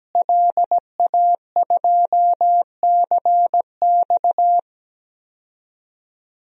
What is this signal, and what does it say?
Morse code, heard as a keyed tone, about 17 wpm, 700 Hz. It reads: LA2CX
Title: DXCC-100 Call Signs - 5 Characters - Repeated 2x - 17wpm